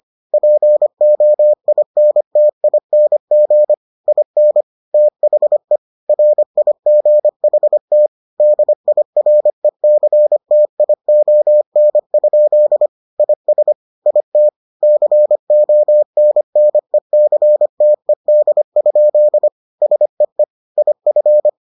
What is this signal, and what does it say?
Morse code, heard as a keyed tone, about 25 wpm, 600 Hz. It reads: POINTING IN THE RIGHT DIRECTION? IS IT CONNECTED? SEE IF